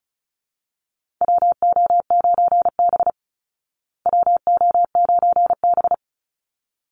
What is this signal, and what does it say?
Morse code, heard as a keyed tone, about 34 wpm, 700 Hz. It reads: WO9B WO9B
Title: Call Signs Repeated 3x - 35wpm